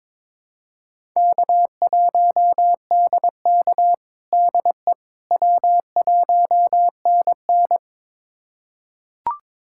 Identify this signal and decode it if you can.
Morse code, heard as a keyed tone, about 22 wpm, 700 Hz. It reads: K1DK DE W1NN
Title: USA Callsign DE Callsign - 4 Characters - 22wpm